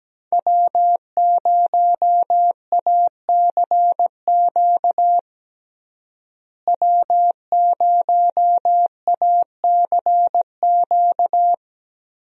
Morse code, heard as a keyed tone, about 17 wpm, 700 Hz. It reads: W0ACQ W0ACQ
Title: USA Call Signs - 5 Characters - Repeated 3x - 17wpm